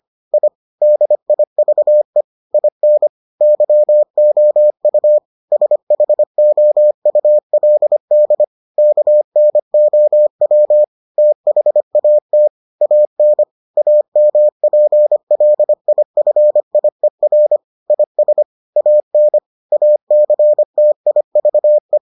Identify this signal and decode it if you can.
Morse code, heard as a keyed tone, about 25 wpm, 600 Hz. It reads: I DIVE IN YOU SHOULD KNOW THAT AN AMPLIFIER IS AN ACTIVE